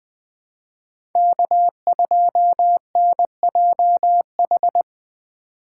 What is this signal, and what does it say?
Morse code, heard as a keyed tone, about 20 wpm, 700 Hz. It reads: K2NJH
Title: USA Call Signs - 2x Character Spacing - 20wpm